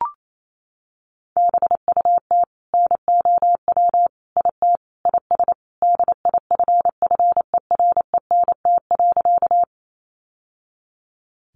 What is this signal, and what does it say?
Morse code, heard as a keyed tone, about 28 wpm, 700 Hz. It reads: BUT NOW IT IS DIFFERENT.
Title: Sentences from Top 600 Words 28wpm